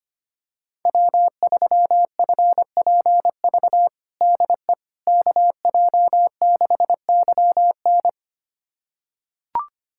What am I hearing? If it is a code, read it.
Morse code, heard as a keyed tone, about 25 wpm, 700 Hz. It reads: W3FPV DE KJ6YN
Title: USA Callsign DE Callsign - 5 Characters - 25wpm